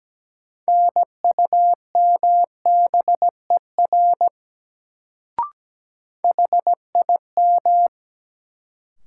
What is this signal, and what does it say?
Morse code, heard as a keyed tone, about 17 wpm, 700 Hz. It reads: NUMBER HIM
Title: Sentences from Top 100 Words - New Word Drill - 17wpm